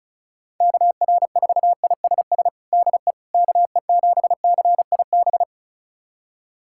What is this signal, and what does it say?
Morse code, heard as a keyed tone, about 36 wpm, 700 Hz. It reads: KR4ISS DE KE7CIB
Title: USA Callsign DE Callsign - 6 Characters - 35wpm